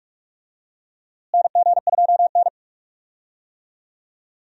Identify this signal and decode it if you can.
Morse code, heard as a keyed tone, about 46 wpm, 700 Hz. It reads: NG2N
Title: USA Call Signs - 4 Characters - Repeated 2x - 45wpm